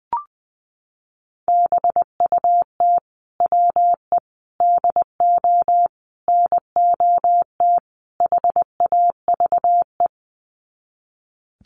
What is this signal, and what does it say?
Morse code, heard as a keyed tone, about 20 wpm, 700 Hz. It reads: BUT WE DO NOT HAVE
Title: Sets of 5 Words Encore2 20wpm